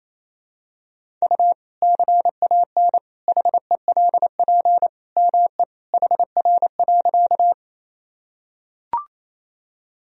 Morse code, heard as a keyed tone, about 27 wpm, 700 Hz. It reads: U CAN HELP ME HR.
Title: Sentences from Top 200 Words and Abbreviations 28wpm